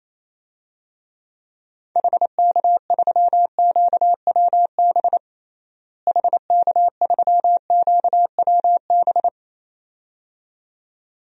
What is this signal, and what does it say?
Morse code, heard as a keyed tone, about 27 wpm, 700 Hz. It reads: HK3QWB HK3QWB
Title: DXCC-100 Call Signs - 6 Characters - Repeated 2x - 28wpm